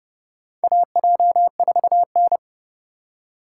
Morse code, heard as a keyed tone, about 30 wpm, 700 Hz. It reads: AJ4N
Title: USA Call Signs - 4 Characters - 2x Character Spacing - 30wpm